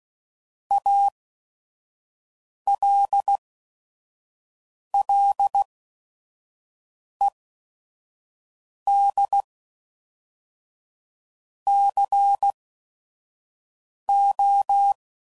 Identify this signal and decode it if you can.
Morse code, heard as a keyed tone, about 16 wpm, 800 Hz. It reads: A L L E D C O